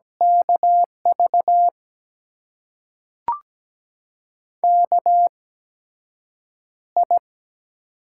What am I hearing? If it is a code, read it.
Morse code, heard as a keyed tone, about 17 wpm, spaced as for 7 wpm, 700 Hz. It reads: KV KI
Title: USA Call Signs - 6 Characters - 8x Character Spacing - 17wpm